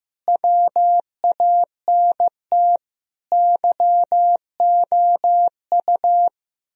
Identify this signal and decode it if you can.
Morse code, heard as a keyed tone, about 15 wpm, 700 Hz. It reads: WANT YOU
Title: Speed-Racing - Sets of 5 Words 15wpm